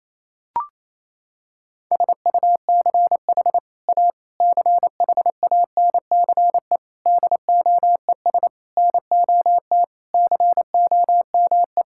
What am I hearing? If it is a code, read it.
Morse code, heard as a keyed tone, about 28 wpm, 700 Hz. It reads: SUCH A CHANCE DOES NOT COME
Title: Sentences from Top 1600 Words 28wpm